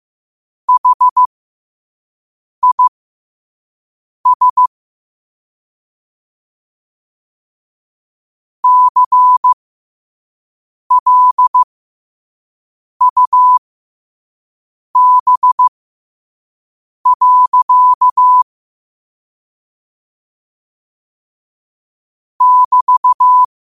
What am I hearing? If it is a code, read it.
Morse code, heard as a keyed tone, about 15 wpm, spaced as for 5 wpm, 1000 Hz. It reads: HIS CLUB. <BT>
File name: qotdfiles / 5wpm